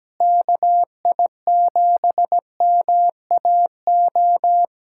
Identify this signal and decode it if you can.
Morse code, heard as a keyed tone, about 17 wpm, 700 Hz. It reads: KI7MAO
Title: USA Callsign DE Callsign 17wpm